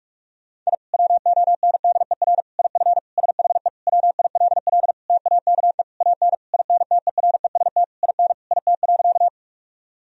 Morse code, heard as a keyed tone, about 41 wpm, 700 Hz. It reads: I WONDER IF SHE WILL TAKE AN INTEREST IN IT.
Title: Sentences from Top 1000 Words - Review Part 3 - 45wpm